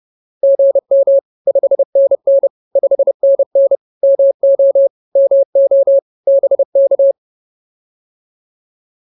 Morse code, heard as a keyed tone, about 30 wpm, 550 Hz. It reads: GM 5NN 5NN MO MO BK